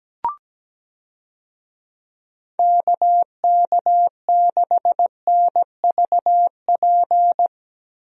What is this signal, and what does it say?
Morse code, heard as a keyed tone, about 17 wpm, 700 Hz. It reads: KK6NVP
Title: USA Call Signs - 6 Characters - Repeated 3x - 17wpm